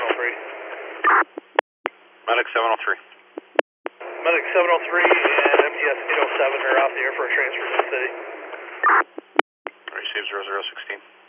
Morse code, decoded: none

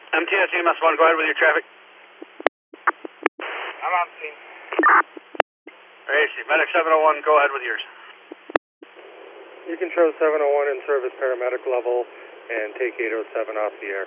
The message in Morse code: none

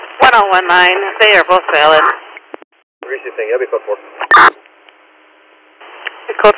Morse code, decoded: none

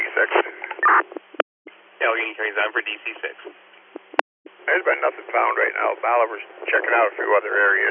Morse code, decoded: none

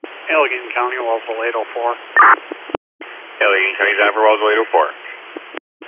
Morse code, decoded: none

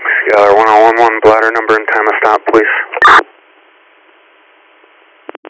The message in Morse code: none

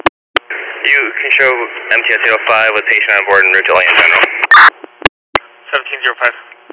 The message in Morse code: none

